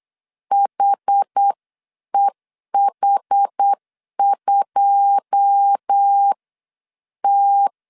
Morse code, heard as a keyed tone, about 8 wpm, spaced as for 7 wpm, 800 Hz. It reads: HEH2T
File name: CW / DET / 1 9 1500C S A 8wpm 5637